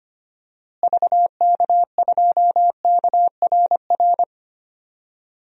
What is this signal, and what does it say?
Morse code, heard as a keyed tone, about 25 wpm, 700 Hz. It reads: VK2KRR
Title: DXCC-100 Call Signs - 6 Characters - Repeated 2x - 25wpm